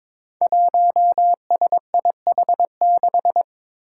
Morse code, heard as a keyed tone, about 22 wpm, 700 Hz. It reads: 1SIH6
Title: Five Challenging Letter-Number Combinations 22wpm